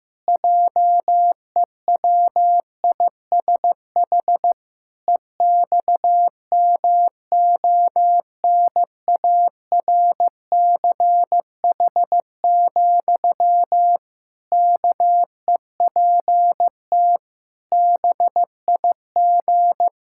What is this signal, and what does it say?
Morse code, heard as a keyed tone, about 15 wpm, 700 Hz. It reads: JEWISH EXMONARCH, KEPT BIG
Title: Pangram Challenge v2 - 15wpm